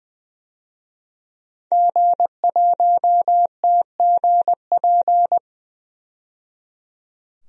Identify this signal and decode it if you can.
Morse code, heard as a keyed tone, about 20 wpm, 700 Hz. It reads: G1TGP
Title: DXCC-100 Call Signs - 5 Characters - 20wpm